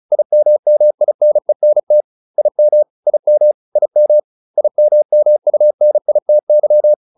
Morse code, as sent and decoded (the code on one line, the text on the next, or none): .. -- -- .. -. -. - / .. -- / .. -- / .. -- / .. -- -- ..- -. .. - -.--
IMMINENT IM IM IM IMMUNITY